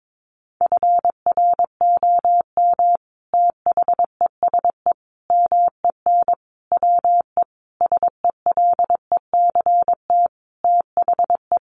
Morse code, heard as a keyed tone, about 22 wpm, 700 Hz. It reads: FROM THESE MEN WE SELECT THE
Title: Sentences from Top 1300 Words 22wpm